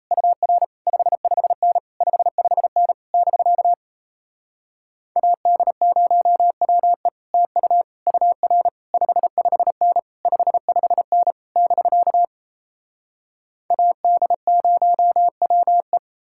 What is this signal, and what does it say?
Morse code, heard as a keyed tone, about 38 wpm, 700 Hz. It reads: UR 55N 55N <BK> AD0WE TU UR 55N 55N <BK> AD0WE